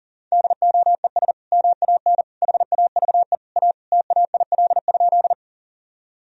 Morse code, heard as a keyed tone, about 37 wpm, 700 Hz. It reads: DOES MAN HAVE A TAIL?
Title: Sentences from Top 1000 Words - Review Part 3 - 40wpm